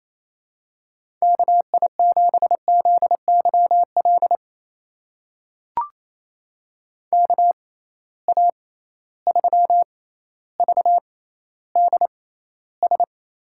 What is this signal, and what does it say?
Morse code, heard as a keyed tone, about 28 wpm, spaced as for 14 wpm, 700 Hz. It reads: KI7ZYL KA3VDS
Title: USA Call Signs - 6 Characters - 6x Character Spacing - 28wpm